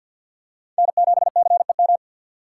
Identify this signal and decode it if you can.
Morse code, heard as a keyed tone, about 52 wpm, 700 Hz. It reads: N7CEM
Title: USA Call Signs - 5 Characters - 6x Character Spacing - 50wpm